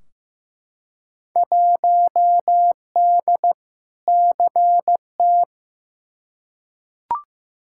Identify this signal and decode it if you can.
Morse code, heard as a keyed tone, about 15 wpm, 700 Hz. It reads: 1D CT
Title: Field Day Exchanges - 15wpm